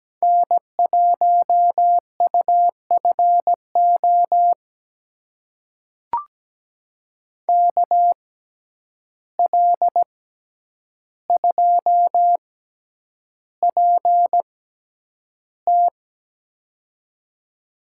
Morse code, heard as a keyed tone, about 17 wpm, spaced as for 9 wpm, 700 Hz. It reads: N1UFO KL2PT
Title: USA Call Signs - 5 Characters - 6x Character Spacing - 17wpm